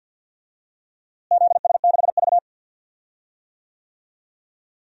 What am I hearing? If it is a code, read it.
Morse code, heard as a keyed tone, about 48 wpm, 700 Hz. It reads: ZS6V
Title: DXCC-100 Call Signs - 4 Characters - Repeated 3x - 50wpm